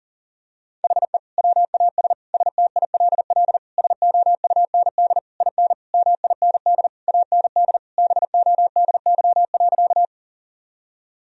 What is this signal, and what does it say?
Morse code, heard as a keyed tone, about 40 wpm, 700 Hz. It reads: HE WAS STILL SOUND IN MIND AND BODY.
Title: Sentences from Top 1000 Words - Review Part 2 - 40wpm